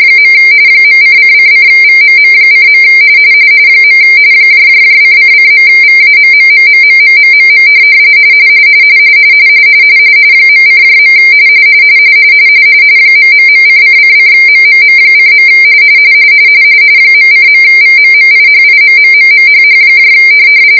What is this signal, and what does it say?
RTTY, about 45.45 baud. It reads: . .::::VMM:. :::.VAMMMMMMMMM .:. ...::::MMM:. .:.VMMIHIMMMMHH